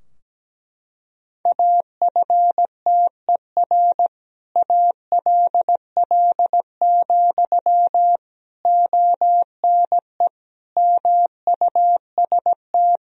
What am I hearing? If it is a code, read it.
Morse code, heard as a keyed tone, about 17 wpm, 700 Hz. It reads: AFTER ALL, ONE MUST
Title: Sentences from Top 1000 Words - Review Part 3 - 17wpm